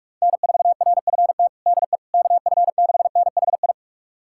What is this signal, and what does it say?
Morse code, heard as a keyed tone, about 41 wpm, 700 Hz. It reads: N4RFT DE KF6NHI
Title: USA Callsign DE Callsign 45wpm